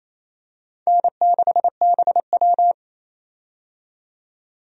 Morse code, heard as a keyed tone, about 28 wpm, 700 Hz. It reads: N6BW
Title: USA Call Signs - 4 Characters - 28wpm